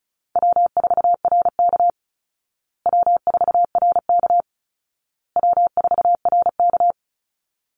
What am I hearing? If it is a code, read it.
Morse code, heard as a keyed tone, about 34 wpm, 700 Hz. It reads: W4RK W4RK W4RK
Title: Call Signs Repeated 2x - 35wpm